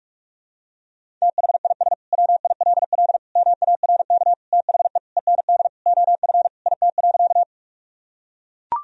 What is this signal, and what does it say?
Morse code, heard as a keyed tone, about 42 wpm, 700 Hz. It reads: THIS WILL MARK THE END OF IT.
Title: Sentences from Top 1000 Words - Review Part 1 - 45wpm